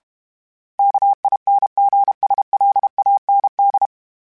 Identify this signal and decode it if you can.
Morse code, heard as a keyed tone, about 32 wpm, 800 Hz. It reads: KINGSLAND
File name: fromSMs / FromSMs mp3s / Cities@30